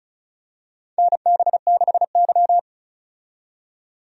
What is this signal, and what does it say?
Morse code, heard as a keyed tone, about 36 wpm, 700 Hz. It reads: NB6Y
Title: USA Call Signs - 4 Characters - 35wpm